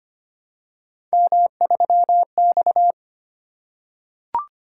Morse code, heard as a keyed tone, about 25 wpm, 700 Hz. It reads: M3X
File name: Three Letter-Number Combinations - 25wpm